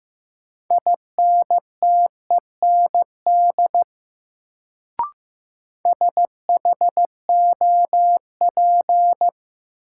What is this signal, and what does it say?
Morse code, heard as a keyed tone, about 15 wpm, 700 Hz. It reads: INTEND SHOP
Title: Sentences from Top 1600 Words - New Word Drill Alt - 15wpm